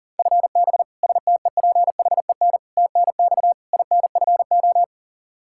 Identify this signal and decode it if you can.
Morse code, heard as a keyed tone, about 40 wpm, 700 Hz. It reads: FB STEPHEN TNX INFO
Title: Practice QSO - Exchange 3 - More Info - 40wpm